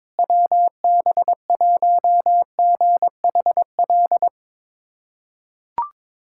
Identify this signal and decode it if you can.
Morse code, heard as a keyed tone, about 22 wpm, 700 Hz. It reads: WB1GHL
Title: USA Callsign DE Callsign - 6 Characters - 22wpm